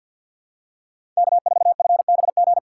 Morse code, heard as a keyed tone, about 46 wpm, 700 Hz. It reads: K4FBG
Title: USA Call Signs - 5 Characters - 2x Character Spacing - 50wpm